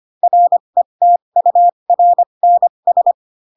Morse code, heard as a keyed tone, about 25 wpm, spaced as for 22 wpm, 700 Hz. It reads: RETURNS
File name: Pinocchio / Pinocchio ch7 18 25 ik1hge